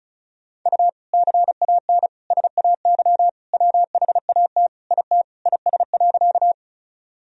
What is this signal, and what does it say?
Morse code, heard as a keyed tone, about 34 wpm, 700 Hz. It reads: U CAN SAY WHAT IT IS.
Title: Sentences from Top 200 Words and Abbreviations 35wpm